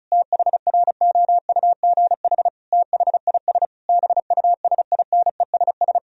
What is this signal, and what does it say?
Morse code, heard as a keyed tone, about 36 wpm, 700 Hz. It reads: THROUGH THIS BUSINESS
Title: Sentences from Top 1000 Words - Review Part 2 - 35wpm